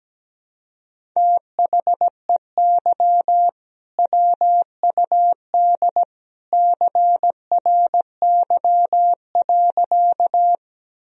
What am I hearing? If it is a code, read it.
Morse code, heard as a keyed tone, about 17 wpm, 700 Hz. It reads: THEY WUD CRY.
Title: Sentences from Top 500 Words and Abbreviations 17wpm